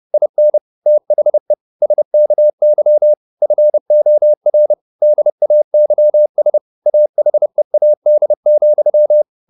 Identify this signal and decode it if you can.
Morse code, heard as a keyed tone, about 30 wpm, 600 Hz. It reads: IN THE SKY FOR DAYS AHEAD,
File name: the old man and the sea / the old man and the sea 1 30wpm